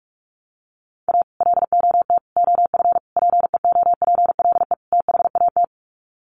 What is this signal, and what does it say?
Morse code, heard as a keyed tone, about 44 wpm, 700 Hz. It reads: A LOT OF PEOPLE THAT
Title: Sets of 5 Words Encore2 45wpm